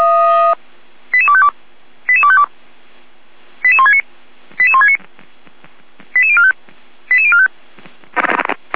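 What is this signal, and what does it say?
Five-tone selective call (ZVEI), heard as quick runs of five tones, tones of about 70 ms. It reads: 88252 88252 88169 88169 88045 88045
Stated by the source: ZVEI five-tone selective call received on an SDR.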